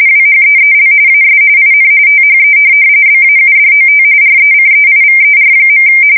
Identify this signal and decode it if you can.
RTTY, about 45.45 baud. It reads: WHEEL AS IN THE BAUDOT RECEIVER.
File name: fsk170 / morkrum